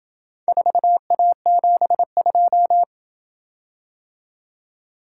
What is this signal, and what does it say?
Morse code, heard as a keyed tone, about 27 wpm, 700 Hz. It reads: 4A72